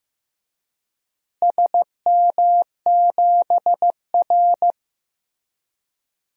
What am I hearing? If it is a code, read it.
Morse code, heard as a keyed tone, about 15 wpm, 700 Hz. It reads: SM7R
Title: DXCC-100 Call Signs - 4 Characters - Repeated 3x - 15wpm